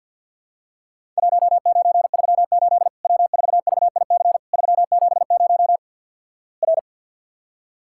Morse code, heard as a keyed tone, about 50 wpm, 700 Hz. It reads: 1938 W4VIC 370 R
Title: Worked All Europe DX Contest - QTC Exchange - 50wpm